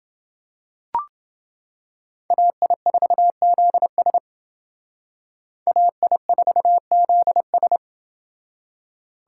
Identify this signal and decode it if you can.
Morse code, heard as a keyed tone, about 29 wpm, 700 Hz. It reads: AI4ZS AI4ZS